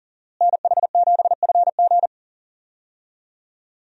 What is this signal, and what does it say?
Morse code, heard as a keyed tone, about 40 wpm, 700 Hz. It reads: NH7FG
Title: USA Call Signs - 5 Characters - Repeated 2x - 40wpm